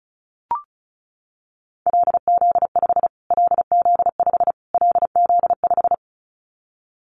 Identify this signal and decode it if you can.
Morse code, heard as a keyed tone, about 34 wpm, 700 Hz. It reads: LZ5 LZ5 LZ5
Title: Callsign Prefixes for Top 25 DXCC - Repeated 3x - 35wpm